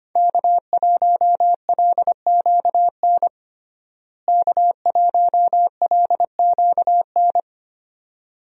Morse code, heard as a keyed tone, about 25 wpm, 700 Hz. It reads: K1LQN K1LQN
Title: USA Call Signs - 5 Characters - Repeated 2x - 25wpm